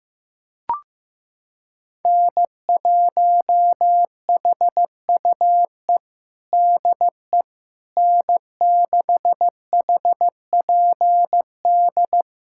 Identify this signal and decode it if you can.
Morse code, heard as a keyed tone, about 16 wpm, 700 Hz. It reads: N1HUE DE N6HPD
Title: USA Callsign DE Callsign 15wpm